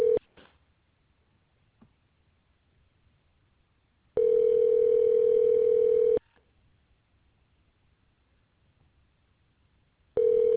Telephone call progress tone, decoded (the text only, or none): ringback tone